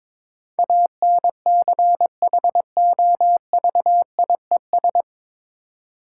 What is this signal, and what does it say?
Morse code, heard as a keyed tone, about 21 wpm, 690 Hz. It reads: ANCHOVIES
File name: Difficult Words with Hints - 22wpm